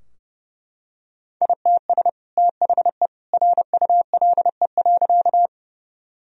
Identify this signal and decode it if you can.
Morse code, heard as a keyed tone, about 32 wpm, 700 Hz. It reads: ITS THE RULE.
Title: Sentences from Top 1000 Words - Review Part 2 - 30wpm